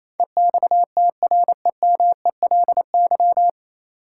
Morse code, heard as a keyed tone, about 28 wpm, 700 Hz. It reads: EXTREMELY
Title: Sentences from Top 1000 Words - Review Part 7 - New Word Drill - 28wpm